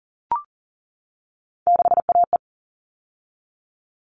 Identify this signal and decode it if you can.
Morse code, heard as a keyed tone, about 40 wpm, 700 Hz. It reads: 6AE
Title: All Cut Numbers - 3-Digit 40wpm